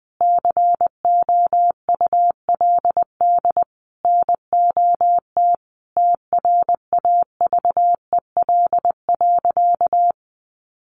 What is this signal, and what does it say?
Morse code, heard as a keyed tone, about 20 wpm, 700 Hz. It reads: COULD NOT TRAVEL.
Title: Sentences from Top 1300 Words 20wpm